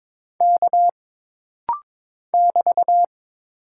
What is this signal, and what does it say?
Morse code, heard as a keyed tone, about 22 wpm, 700 Hz. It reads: K <BT>
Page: This rapid-fire timing is geared to push your ICR (Instant Character Recognition) skills.